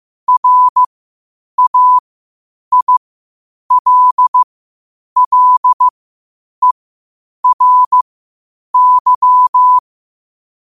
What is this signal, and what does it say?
Morse code, heard as a keyed tone, about 15 wpm, spaced as for 8 wpm, 1000 Hz. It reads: RAILLERY
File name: qotdfiles / 8wpm